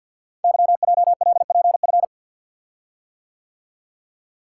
Morse code, heard as a keyed tone, about 46 wpm, 700 Hz. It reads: YJLPF
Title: Code Groups - 5 Letters - 50wpm